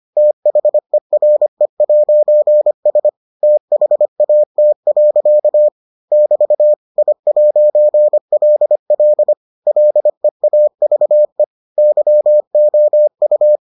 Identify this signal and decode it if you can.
Morse code, heard as a keyed tone, about 25 wpm, 600 Hz. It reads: THERE'S THAT. <BT> I'LL LEAVE YOU